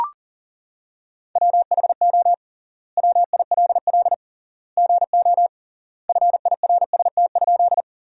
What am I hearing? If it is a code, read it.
Morse code, heard as a keyed tone, about 37 wpm, 700 Hz. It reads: WHO WILL GO FIRST?
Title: 3x Word-Spacing - Sentences from Top 100 Words - 40wpm